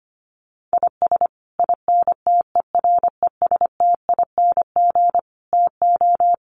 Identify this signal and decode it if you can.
Morse code, heard as a keyed tone, about 25 wpm, 700 Hz. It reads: IS INTERESTING TO
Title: Sets of 5 Words Encore2 25wpm